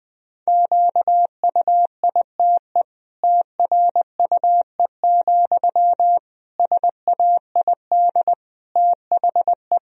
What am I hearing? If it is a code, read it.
Morse code, heard as a keyed tone, about 21 wpm, 700 Hz. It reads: QUITE TRUE, SAID THE
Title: Sentences from Top 1000 Words - Review Part 3 - 20wpm